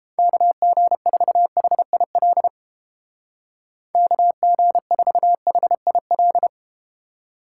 Morse code, heard as a keyed tone, about 31 wpm, 700 Hz. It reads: KG4HIL KG4HIL